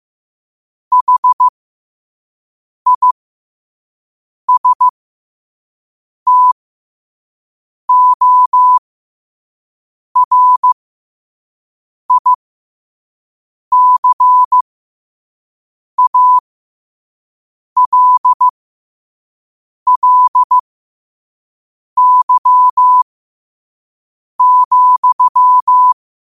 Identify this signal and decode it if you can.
Morse code, heard as a keyed tone, about 15 wpm, spaced as for 5 wpm, 1000 Hz. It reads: HISTORICALLY,
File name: qotdfiles / 5wpm